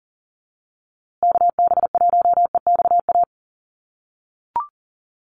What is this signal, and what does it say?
Morse code, heard as a keyed tone, about 40 wpm, 700 Hz. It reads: KB1EXA
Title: USA Call Signs Repeated 3x 40wpm